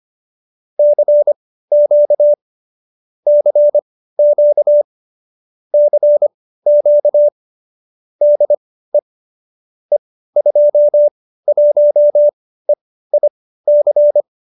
Morse code, heard as a keyed tone, about 25 wpm, spaced as for 15 wpm, 570 Hz. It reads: CQ CQ CQ DE E21EIC